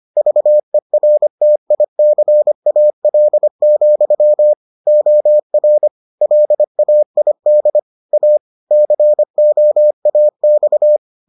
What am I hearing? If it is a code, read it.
Morse code, heard as a keyed tone, about 25 wpm, 600 Hz. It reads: VERTICAL, OR LAID A COAX